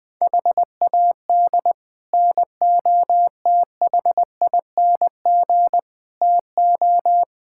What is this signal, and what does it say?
Morse code, heard as a keyed tone, about 21 wpm, 700 Hz. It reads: HAD NOTHING TO
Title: Build A Phrase - 4-Word Phrases - 20wpm